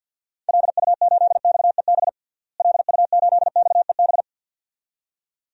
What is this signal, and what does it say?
Morse code, heard as a keyed tone, about 52 wpm, 700 Hz. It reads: LU8XEB LU8XEB
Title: DXCC-100 Call Signs - 6 Characters - Repeated 2x - 50wpm